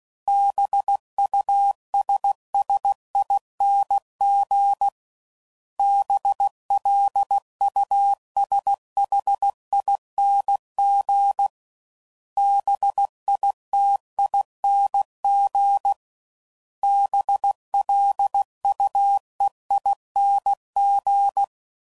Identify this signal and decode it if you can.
Morse code, heard as a keyed tone, about 16 wpm, 800 Hz. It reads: BUSSING BLUSHING BITING BLUEING
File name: Suffixes / Special INGs / Sngl Ings